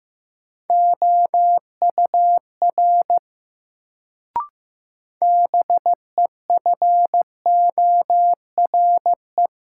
Morse code, heard as a keyed tone, about 15 wpm, 700 Hz. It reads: OUR BEFORE
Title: Sentences from Top 200 Words - New Word Drill - 15wpm